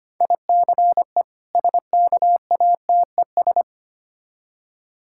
Morse code, heard as a keyed tone, about 25 wpm, 700 Hz. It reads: ICE SKATES
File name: practice / 25wpm / Difficult Words with Hints - 25wpm